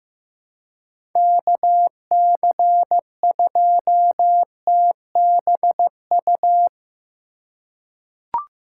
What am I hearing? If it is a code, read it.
Morse code, heard as a keyed tone, about 15 wpm, 700 Hz. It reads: KC2TBU
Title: USA Call Signs - 2x Character Spacing - 15wpm